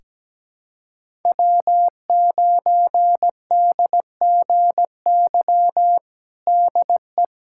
Morse code, heard as a keyed tone, about 17 wpm, 700 Hz. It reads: W9DGY DE
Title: USA Callsign DE Callsign - 5 Characters - 17wpm